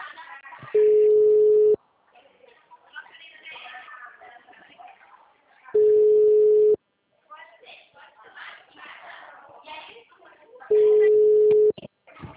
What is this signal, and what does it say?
Ringback tone (Europe, 425 Hz), with speech.
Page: Leme 2